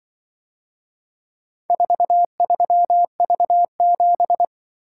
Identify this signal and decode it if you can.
Morse code, heard as a keyed tone, about 23 wpm, 700 Hz. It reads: 43V7
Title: Speed-Racing - Four Letter-Number Combinations - 20wpm